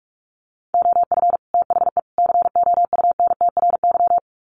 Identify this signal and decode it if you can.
Morse code, heard as a keyed tone, about 44 wpm, 700 Hz. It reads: OF THE COUNTRY
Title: Sets of 3 Words 45wpm